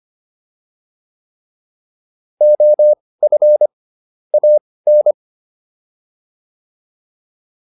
Morse code, heard as a keyed tone, about 25 wpm, spaced as for 18 wpm, 600 Hz. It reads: OF AN